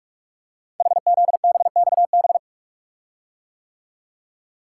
Morse code, heard as a keyed tone, about 44 wpm, 690 Hz. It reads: HZBXB
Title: Code Groups - 5 Letters - 45wpm